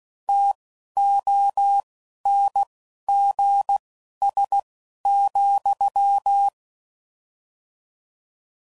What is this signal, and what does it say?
Morse code, heard as a keyed tone, about 16 wpm, spaced as for 11 wpm, 800 Hz. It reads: TONGS,
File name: Mixed / Misc / WordPairs